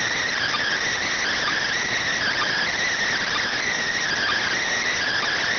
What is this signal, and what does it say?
Emergency Alert System (SAME) tones, heard as harsh buzzing bursts. It is heard from 1.0 to 4.6 s.